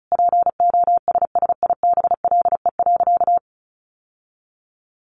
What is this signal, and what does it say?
Morse code, heard as a keyed tone, about 34 wpm, 700 Hz. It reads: POSSIBLE.
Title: Sentences from Top 1100 Words 35wpm